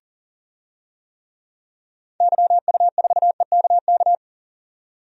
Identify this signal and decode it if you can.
Morse code, heard as a keyed tone, about 40 wpm, 700 Hz. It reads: YU4EKK
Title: DXCC-100 Call Signs - 6 Characters - 40wpm